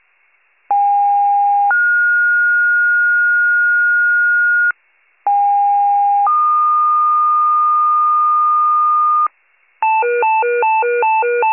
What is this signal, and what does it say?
Two-tone page, heard as a one-second tone, then a three-second tone, about 810 then 1430 Hz, about 810 then 1220 Hz, with noise. It is heard from 0.7 to 9.3 s.